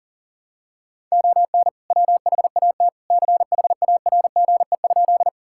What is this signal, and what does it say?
Morse code, heard as a keyed tone, about 40 wpm, 700 Hz. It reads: ON WHAT CHARGE?
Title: Sentences from Top 1000 Words - Review Part 6 - 40wpm